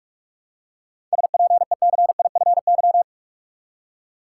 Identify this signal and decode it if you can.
Morse code, heard as a keyed tone, about 47 wpm, 700 Hz. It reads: SPECIFY
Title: Seven Letter Words - 45wpm